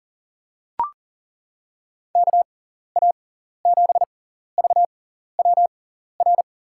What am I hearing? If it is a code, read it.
Morse code, heard as a keyed tone, about 40 wpm, spaced as for 14 wpm, 700 Hz. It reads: KA7VWR
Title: USA Call Signs - 6 Characters - 6x Character Spacing - 40wpm